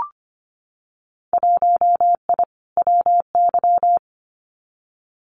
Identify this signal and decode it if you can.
Morse code, heard as a keyed tone, about 25 wpm, 690 Hz. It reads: 1I WY
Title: Winter Field Day Exchanges Encore 25wpm